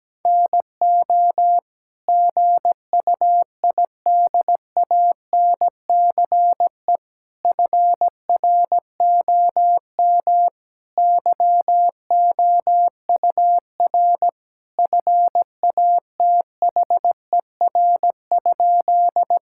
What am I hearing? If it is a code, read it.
Morse code, heard as a keyed tone, about 17 wpm, 700 Hz. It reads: NO GUIDANCE FROM YOUR FATHER?
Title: Sentences from Top 1600 Words 17wpm